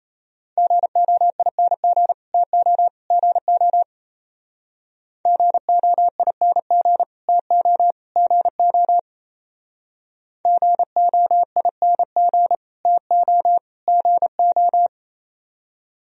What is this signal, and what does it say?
Morse code, heard as a keyed tone, about 35 wpm, 700 Hz. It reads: GOING TO GO GOING TO GO GOING TO GO
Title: Speed-Racing - Sets of 3 Words 28wpm